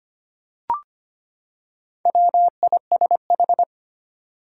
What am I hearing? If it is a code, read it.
Morse code, heard as a keyed tone, about 25 wpm, 700 Hz. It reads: WISH
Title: Four Letter Words - 25wpm